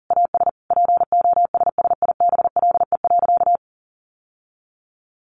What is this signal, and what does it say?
Morse code, heard as a keyed tone, about 40 wpm, 700 Hz. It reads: AS POSSIBLE.
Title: Sentences from Top 900 Words 40wpm